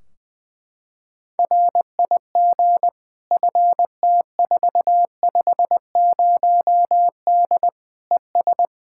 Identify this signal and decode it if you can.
Morse code, heard as a keyed tone, about 21 wpm, 700 Hz. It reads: RIG FT450D ES
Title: Practice QSO - Exchange 2 - Info - 20wpm